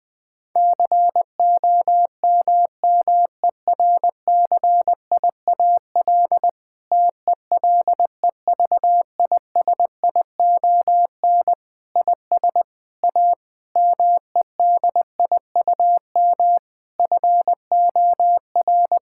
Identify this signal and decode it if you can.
Morse code, heard as a keyed tone, about 21 wpm, 700 Hz. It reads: COMMERCIAL TELEVISION IS A MEDIUM FOR
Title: Sentences from Top 1000 Words - Review Part 6 - 20wpm